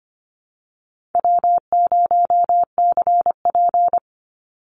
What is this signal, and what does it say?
Morse code, heard as a keyed tone, about 25 wpm, 700 Hz. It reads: W0CP
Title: NA SOTA Activator Callsigns 25wpm